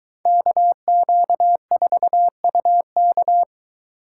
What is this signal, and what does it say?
Morse code, heard as a keyed tone, about 22 wpm, 700 Hz. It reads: KQ4UK